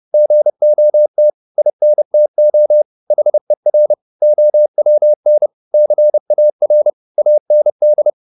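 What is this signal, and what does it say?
Morse code, heard as a keyed tone, about 30 wpm, 600 Hz. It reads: GOT INTO HER OWN CAR AND
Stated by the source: Morse code at 30 wpm, the novel Never by Ken Follett in English, from CW med Gnister.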